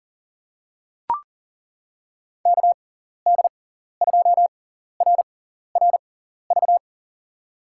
Morse code, heard as a keyed tone, about 39 wpm, spaced as for 14 wpm, 700 Hz. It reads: KD2RRV
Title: USA Call Signs - 6 Characters - 6x Character Spacing - 40wpm